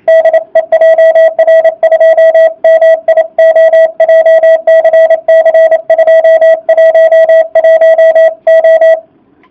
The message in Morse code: DE JR2MIO JCC 211O

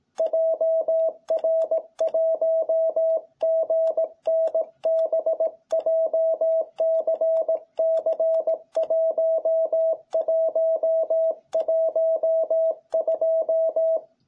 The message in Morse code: JR1GNB JCC1112

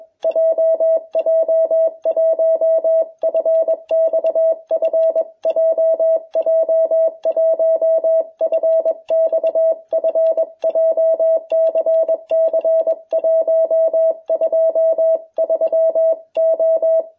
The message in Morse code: JJ1FXF JJ1FXF JCC 123O

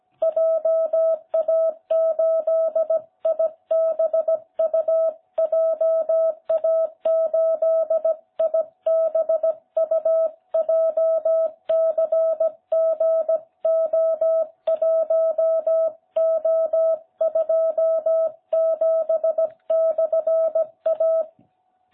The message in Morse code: JA8IBU JA8IBU JCGO1O27/A

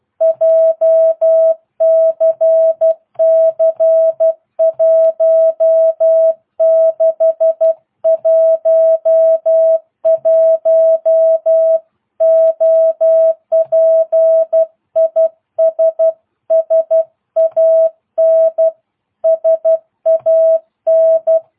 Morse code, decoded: JCC 1611 OP IS SAN SAN

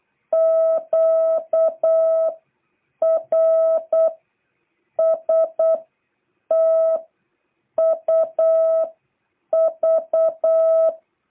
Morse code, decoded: Q R S T U V

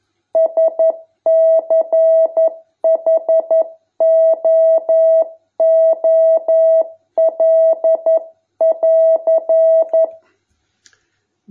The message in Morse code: SCHOOL<AR>

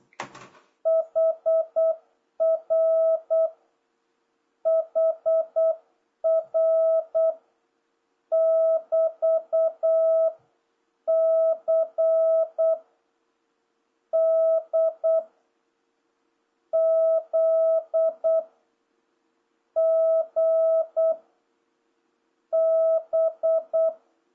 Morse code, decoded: HR HR <BT> C D Z G B

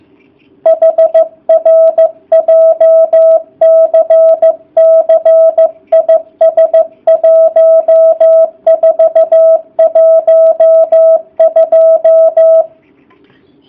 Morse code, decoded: HR JCC IS 1412

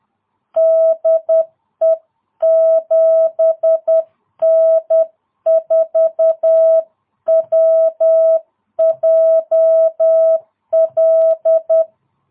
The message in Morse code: DE 7N4WJL

